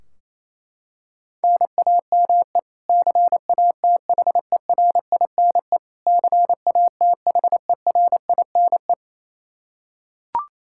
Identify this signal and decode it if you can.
Morse code, heard as a keyed tone, about 28 wpm, 700 Hz. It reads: NAME CATHERINE CATHERINE
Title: Practice QSO - Exchange 1 - Beginning - 28wpm